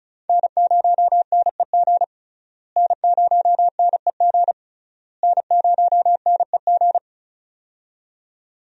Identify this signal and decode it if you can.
Morse code, heard as a keyed tone, about 36 wpm, 700 Hz. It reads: N0NEG N0NEG N0NEG
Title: USA Call Signs - 5 Characters - Repeated 3x - 35wpm